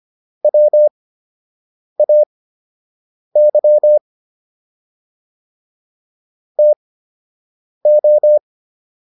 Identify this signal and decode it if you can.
Morse code, heard as a keyed tone, about 25 wpm, spaced as for 7 wpm, 600 Hz. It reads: WAY TO